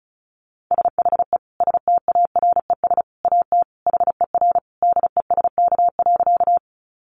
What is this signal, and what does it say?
Morse code, heard as a keyed tone, about 34 wpm, 700 Hz. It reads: SHE STARES AT HER DESK.
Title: Easy Phrases 35wpm